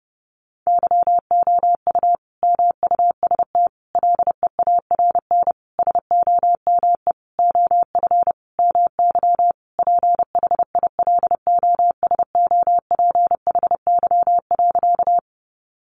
Morse code, heard as a keyed tone, about 28 wpm, 700 Hz. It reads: YOU MUST LEARN SOME OF MY PHILOSOPHY.
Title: Sentences from Top 900 Words 30wpm